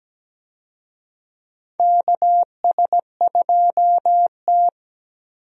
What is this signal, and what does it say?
Morse code, heard as a keyed tone, about 17 wpm, 700 Hz. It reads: KS2T
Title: USA Call Signs - 4 Characters - 17wpm